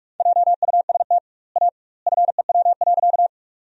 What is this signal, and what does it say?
Morse code, heard as a keyed tone, about 47 wpm, 700 Hz. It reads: JUST A FEW.
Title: 2x Word-Spacing - Sentences from Top 300 Words - 45wpm